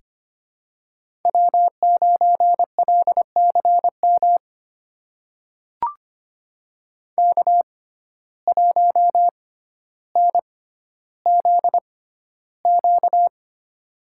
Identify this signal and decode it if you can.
Morse code, heard as a keyed tone, about 26 wpm, spaced as for 13 wpm, 700 Hz. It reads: W9LCM K1NZQ
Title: USA Call Signs - 5 Characters - 6x Character Spacing - 25wpm